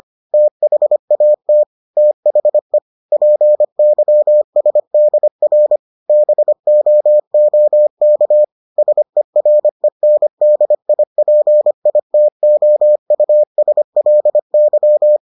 Morse code, heard as a keyed tone, about 25 wpm, 600 Hz. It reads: THAT THE PYSDR BOOK SERENDIPITOUSLY